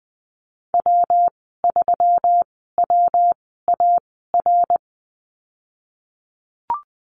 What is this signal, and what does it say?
Morse code, heard as a keyed tone, about 19 wpm, spaced as for 14 wpm, 700 Hz. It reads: W3WAR
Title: 173 Lesson 28g - Number 9 - Call Signs 20wpm